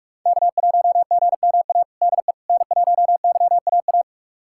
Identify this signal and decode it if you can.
Morse code, heard as a keyed tone, about 46 wpm, 700 Hz. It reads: K1GMA DE N1YAA
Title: USA Callsign DE Callsign - 5 Characters - 45wpm